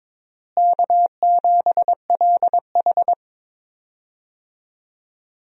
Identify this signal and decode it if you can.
Morse code, heard as a keyed tone, about 21 wpm, 700 Hz. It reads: K7LH
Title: USA Call Signs - 4 Characters - Repeated 2x - 22wpm